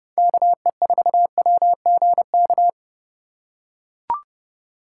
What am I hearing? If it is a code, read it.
Morse code, heard as a keyed tone, about 30 wpm, 700 Hz. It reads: KE4WGK
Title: USA Call Signs - 6 Characters - 3x Character Spacing - 30wpm